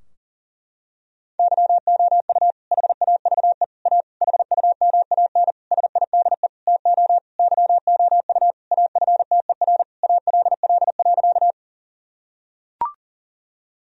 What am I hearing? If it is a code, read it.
Morse code, heard as a keyed tone, about 40 wpm, 700 Hz. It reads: YOU HAVE A HUMAN SIDE TO YOU AFTER ALL.
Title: Sentences from Top 1000 Words - Review Part 3 - 40wpm